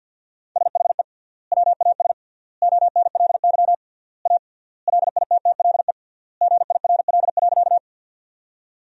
Morse code, heard as a keyed tone, about 50 wpm, 700 Hz. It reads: SHE WAS ONLY A LITTLE GIRL.
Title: 3x Word-Spacing - Sentences from Top 400 Words - 50wpm